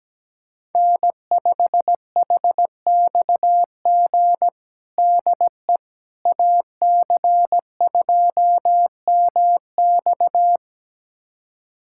Morse code, heard as a keyed tone, about 17 wpm, 700 Hz. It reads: N5HXG DE AC2MX
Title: USA Callsign DE Callsign - 5 Characters - 17wpm